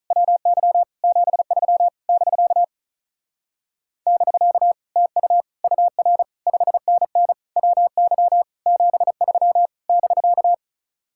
Morse code, heard as a keyed tone, about 45 wpm, 700 Hz. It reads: WY 73 <BK> <BK> TU UR 5NN WY 73 <BK>